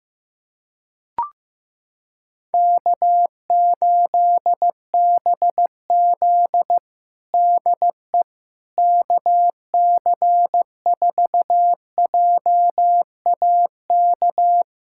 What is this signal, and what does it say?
Morse code, heard as a keyed tone, about 15 wpm, 700 Hz. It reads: K8BZ DE KC4JAK
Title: Practice QSO - Exchange 4 - Ending - 15wpm